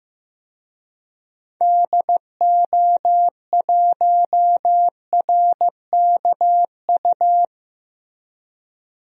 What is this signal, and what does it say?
Morse code, heard as a keyed tone, about 15 wpm, 700 Hz. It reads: DO1RKU
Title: DXCC-100 Call Signs - 6 Characters - 15wpm